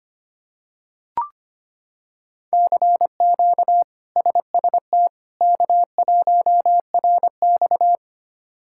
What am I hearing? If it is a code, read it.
Morse code, heard as a keyed tone, about 25 wpm, 700 Hz. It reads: CQ SST K1RX
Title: K1USN SST QSO Practice - 25wpm